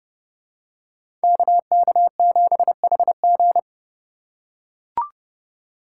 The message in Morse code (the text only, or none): KK7HG